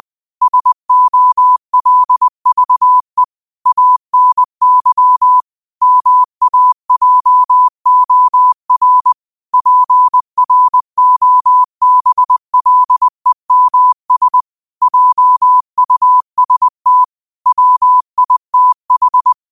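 Morse code, heard as a keyed tone, about 20 wpm, 1000 Hz. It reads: SOLVE ANY MAJOR PROBLEMS JUST WITH
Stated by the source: Morse quote practice at 20 wpm.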